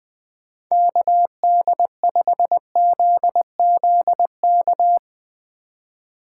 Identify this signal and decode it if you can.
Morse code, heard as a keyed tone, about 21 wpm, 700 Hz. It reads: KD5ZZK
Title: 2023 SOTA Activator Call Signs - 20wpm